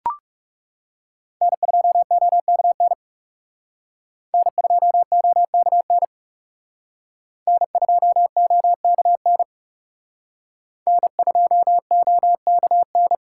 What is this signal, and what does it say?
Morse code, heard as a keyed tone, about 38 wpm, 700 Hz. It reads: N2OKN N2OKN N2OKN N2OKN